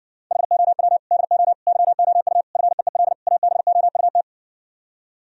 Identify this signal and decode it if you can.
Morse code, heard as a keyed tone, about 60 wpm, 700 Hz. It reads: HOW DO YOU FEEL ABOUT